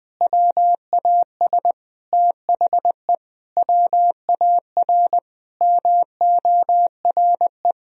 Morse code, heard as a keyed tone, about 20 wpm, 700 Hz. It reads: WAS THE WAR MORE
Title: Sentences from Top 1000 Words - Review Part 3 - 20wpm